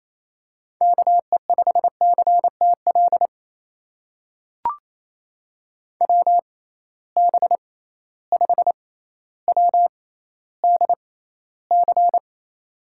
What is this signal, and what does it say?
Morse code, heard as a keyed tone, about 28 wpm, spaced as for 14 wpm, 700 Hz. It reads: KE5CTL WB5WDC
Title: USA Call Signs - 6 Characters - 6x Character Spacing - 28wpm